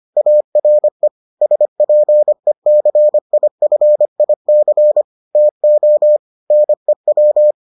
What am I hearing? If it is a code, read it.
Morse code, heard as a keyed tone, about 25 wpm, 600 Hz. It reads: ARE SPECIFIC TO NEW